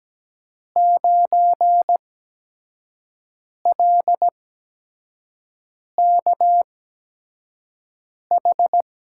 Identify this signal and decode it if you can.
Morse code, heard as a keyed tone, about 17 wpm, spaced as for 5 wpm, 700 Hz. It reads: 9LKH
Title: USA Call Signs - 8x Character Spacing - 17wpm